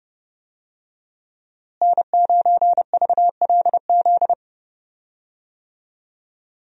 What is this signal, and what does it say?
Morse code, heard as a keyed tone, about 30 wpm, 700 Hz. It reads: N9VLZ
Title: USA Call Signs - 5 Characters - Repeated 2x - 30wpm